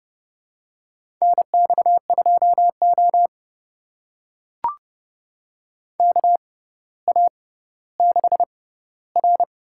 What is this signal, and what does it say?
Morse code, heard as a keyed tone, about 32 wpm, spaced as for 16 wpm, 700 Hz. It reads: NX2O KA6R
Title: USA Call Signs - 4 Characters - 6x Character Spacing - 30wpm